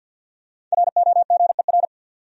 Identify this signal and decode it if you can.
Morse code, heard as a keyed tone, about 46 wpm, 700 Hz. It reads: ROGER
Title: USA Common Names - 50wpm